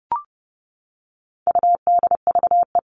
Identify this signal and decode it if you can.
Morse code, heard as a keyed tone, about 30 wpm, 700 Hz. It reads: UDVE
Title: All Cut Numbers - 4-Digit 30wpm